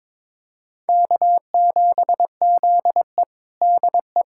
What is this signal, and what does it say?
Morse code, heard as a keyed tone, about 22 wpm, 700 Hz. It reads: K7ZE DE
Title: USA Callsign DE Callsign - 4 Characters - 22wpm